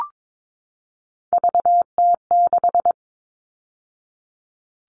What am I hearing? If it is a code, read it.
Morse code, heard as a keyed tone, about 22 wpm, 700 Hz. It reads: VT6
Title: All Cut Numbers - 3-Digit 22wpm